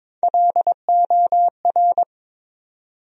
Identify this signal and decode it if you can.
Morse code, heard as a keyed tone, about 22 wpm, 700 Hz. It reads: LOR
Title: USA Call Sign Postfixes - 3 Letters 22wpm